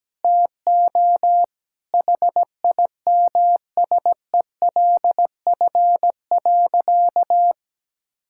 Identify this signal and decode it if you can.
Morse code, heard as a keyed tone, about 17 wpm, 700 Hz. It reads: TO HIMSELF.
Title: Sentences from Top 1600 Words 17wpm